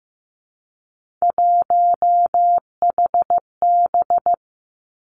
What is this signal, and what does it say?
Morse code, heard as a keyed tone, about 15 wpm, 700 Hz. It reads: 1HB
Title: Three Challenging Letter-Number Combinations 15wpm